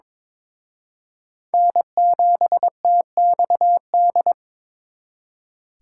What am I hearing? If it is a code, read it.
Morse code, heard as a keyed tone, about 26 wpm, 700 Hz. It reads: N7TXD
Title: Speed Racing - USA 1x3 Call Signs 22wpm